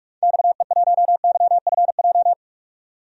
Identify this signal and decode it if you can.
Morse code, heard as a keyed tone, about 46 wpm, 700 Hz. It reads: XE1YFJ
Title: DXCC-100 Call Signs - 6 Characters - 45wpm